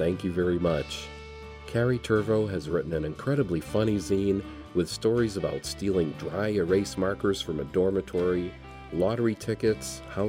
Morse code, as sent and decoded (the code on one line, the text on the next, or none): none